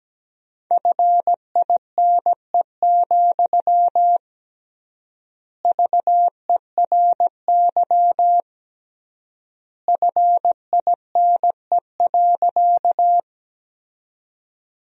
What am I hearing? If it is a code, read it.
Morse code, heard as a keyed tone, about 17 wpm, 700 Hz. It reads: FINE, VERY FINE.
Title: 3x Word-Spacing - Sentences from Top 500 Words - 17wpm